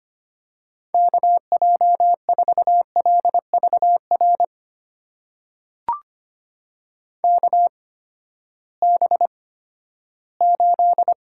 Morse code, heard as a keyed tone, about 25 wpm, spaced as for 14 wpm, 700 Hz. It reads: KJ4LVR KB8
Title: USA Call Signs - 6 Characters - 8x Character Spacing - 25wpm